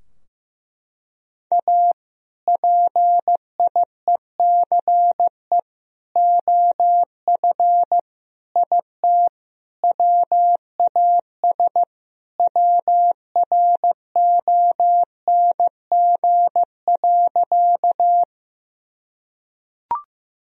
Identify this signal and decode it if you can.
Morse code, heard as a keyed tone, about 15 wpm, 700 Hz. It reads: A PIECE OF IT WAS WRONG.
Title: Sentences from Top 1000 Words - Review Part 2 - 15wpm